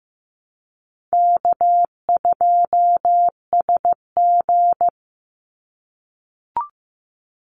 Morse code, heard as a keyed tone, about 15 wpm, 700 Hz. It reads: K2SG
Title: Easy Call Signs - 15wpm